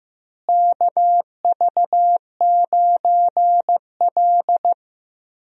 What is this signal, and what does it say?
Morse code, heard as a keyed tone, about 15 wpm, 700 Hz. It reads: KV9L
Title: USA Call Signs - 4 Characters - Repeated 2x - 15wpm